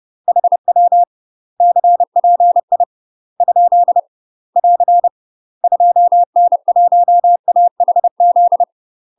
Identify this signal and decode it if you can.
Morse code, heard as a keyed tone, about 30 wpm, 700 Hz. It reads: HW CPI ? <AR> 2N1AHZ
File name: UK MASTER QSOs TRAINING/30WPM